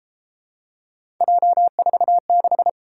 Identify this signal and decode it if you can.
Morse code, heard as a keyed tone, about 32 wpm, 700 Hz. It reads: J46